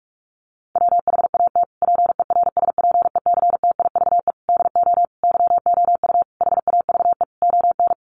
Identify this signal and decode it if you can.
Morse code, heard as a keyed tone, about 44 wpm, 700 Hz. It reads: WHAT PERSPECTIVE DO YOU HAVE ON
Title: Sentences from Top 1200 Words 45wpm